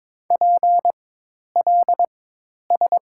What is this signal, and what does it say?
Morse code, heard as a keyed tone, about 21 wpm, spaced as for 10 wpm, 700 Hz. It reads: PLS
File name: Top 90 QSO Elements - 4x Character Spacing - 22wpm